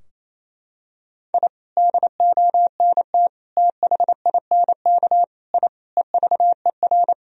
Morse code, heard as a keyed tone, about 28 wpm, 700 Hz. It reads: I DONT THINK I EVER
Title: Sentences from Top 1000 Words - Review Part 4 - 28wpm